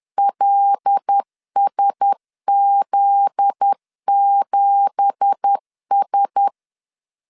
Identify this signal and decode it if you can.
Morse code, heard as a keyed tone, about 10 wpm, 800 Hz. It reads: LSZ7S
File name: CW / DET / 1 9 1500C S A 10wpm 4511